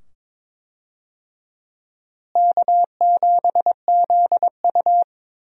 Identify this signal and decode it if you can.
Morse code, heard as a keyed tone, about 22 wpm, 700 Hz. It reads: K7ZU
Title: USA Call Signs - 4 Characters - Repeated 2x - 22wpm